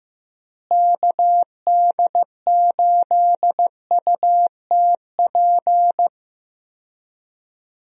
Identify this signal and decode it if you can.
Morse code, heard as a keyed tone, about 15 wpm, 700 Hz. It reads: KD8UTP
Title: USA Call Signs - 6 Characters - Repeated 2x - 15wpm